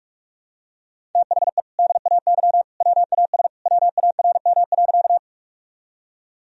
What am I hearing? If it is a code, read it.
Morse code, heard as a keyed tone, about 47 wpm, 700 Hz. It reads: THE DAY WAS WARM.
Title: Sentences from Top 1000 Words - Review Part 2 - 45wpm